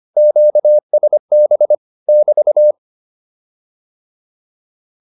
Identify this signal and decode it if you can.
Morse code, heard as a keyed tone, about 25 wpm, 600 Hz. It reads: QSB <BT>
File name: QSO105 25